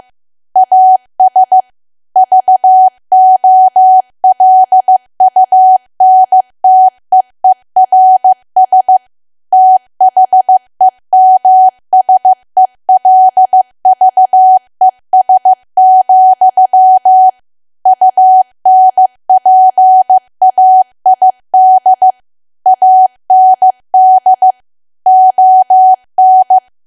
Morse code, heard as a keyed tone, about 15 wpm, 750 Hz. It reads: AS VOLUNTEERS THEMSELVES, UNPAID AND ON
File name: cwa / W1AW-Level2 / 15WPM